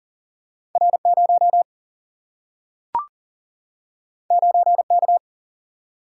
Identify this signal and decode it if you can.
Morse code, heard as a keyed tone, about 40 wpm, 700 Hz. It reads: R0 9K
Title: Single-Letter Single-Number 40wpm